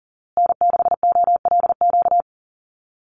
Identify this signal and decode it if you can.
Morse code, heard as a keyed tone, about 40 wpm, 700 Hz. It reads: N6OLQ
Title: USA Call Signs Repeated 3x 40wpm